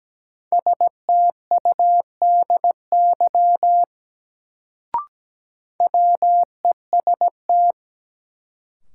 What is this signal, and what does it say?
Morse code, heard as a keyed tone, about 17 wpm, 700 Hz. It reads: STUDY WEST
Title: Sets of 5 Words - New Word Drill - 17wpm